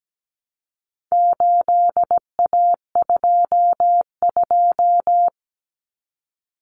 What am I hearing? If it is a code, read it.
Morse code, heard as a keyed tone, about 17 wpm, 700 Hz. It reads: 8A22
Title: Common Cut Numbers - 4-Digit 17wpm